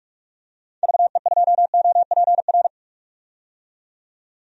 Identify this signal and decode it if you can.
Morse code, heard as a keyed tone, about 43 wpm, 700 Hz. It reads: VE2OPR
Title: DXCC-100 Call Signs - 6 Characters - 45wpm